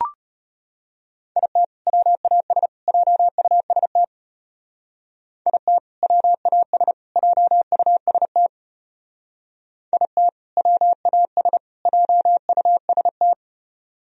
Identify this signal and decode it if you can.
Morse code, heard as a keyed tone, about 31 wpm, 700 Hz. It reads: IT WAS JUST IT WAS JUST IT WAS JUST